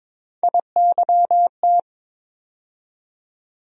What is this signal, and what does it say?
Morse code, heard as a keyed tone, about 22 wpm, 700 Hz. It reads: IYT
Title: USA Call Sign Postfixes - 3 Letters 22wpm